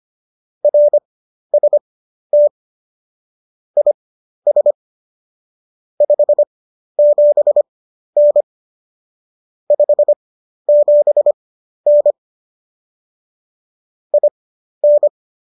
Morse code, heard as a keyed tone, about 25 wpm, spaced as for 12 wpm, 590 Hz. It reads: RST IS 57N 57N IN